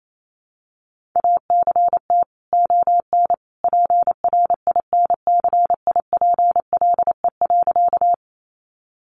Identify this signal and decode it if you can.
Morse code, heard as a keyed tone, about 28 wpm, 700 Hz. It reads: ACT ON PRINCIPLE.
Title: Sentences from Top 1400 Words 28wpm